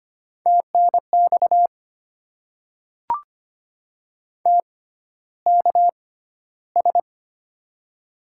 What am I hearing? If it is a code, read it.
Morse code, heard as a keyed tone, about 25 wpm, spaced as for 13 wpm, 690 Hz. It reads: TNX TKS
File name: Top 90 QSO Elements - 6x Character Spacing - 25wpm